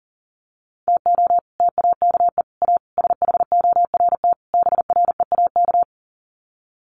Morse code, heard as a keyed tone, about 40 wpm, 700 Hz. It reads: TO TAKE A SHORT BREAK
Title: Sets of 5 Words Encore2 40wpm